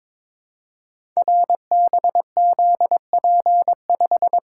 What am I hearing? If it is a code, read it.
Morse code, heard as a keyed tone, about 22 wpm, 700 Hz. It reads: RBZP5
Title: Code Groups - 5 Letter:Numbers - 22wpm